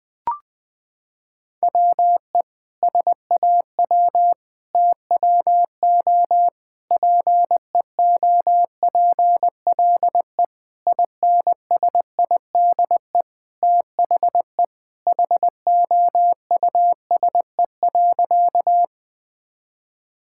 Morse code, heard as a keyed tone, about 20 wpm, 700 Hz. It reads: WE SAW TWO PEOPLE INSIDE THE HOUSE.
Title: Sentences from Top 1000 Words - Review Part 6 - 20wpm